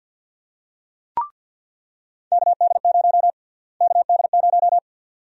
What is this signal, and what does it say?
Morse code, heard as a keyed tone, about 48 wpm, 700 Hz. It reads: KD0 KD0
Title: USA Call Sign Prefixes - Repeated 2x - 50wpm